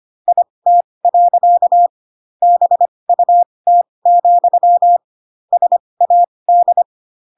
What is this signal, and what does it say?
Morse code, heard as a keyed tone, about 25 wpm, spaced as for 20 wpm, 700 Hz. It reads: IT. BUT, SAD
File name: Pinocchio / Pinocchio ch4 15 25 ik1hge